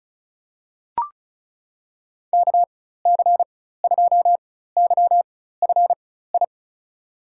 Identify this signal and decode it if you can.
Morse code, heard as a keyed tone, about 36 wpm, spaced as for 16 wpm, 700 Hz. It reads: KC2YFI
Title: USA Call Signs - 4x Character Spacing - 35wpm